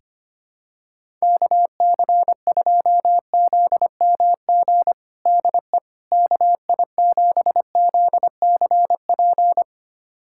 Morse code, heard as a keyed tone, about 25 wpm, 700 Hz. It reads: KC2ZMG DE KI7ZCP
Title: USA Callsign DE Callsign - 6 Characters - 25wpm